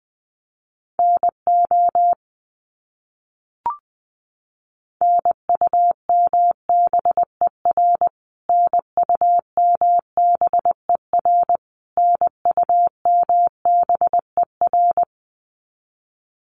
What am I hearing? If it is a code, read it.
Morse code, heard as a keyed tone, about 20 wpm, 700 Hz. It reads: NO NUMBER NUMBER NUMBER
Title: Top 300 Words Repeated 20wpm